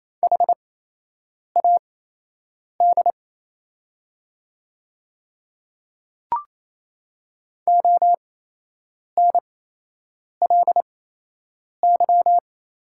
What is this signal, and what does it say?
Morse code, heard as a keyed tone, about 28 wpm, spaced as for 8 wpm, 700 Hz. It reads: HAD ONLY
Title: 8x Character Spacing - Top 200 Words - 28wpm